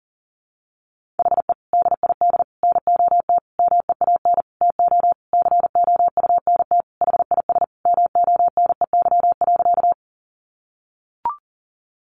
Morse code, heard as a keyed tone, about 40 wpm, 700 Hz. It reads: HE DID NOT MEAN TO COUNT HIS MONEY.
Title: Sentences from Top 1500 Words 40wpm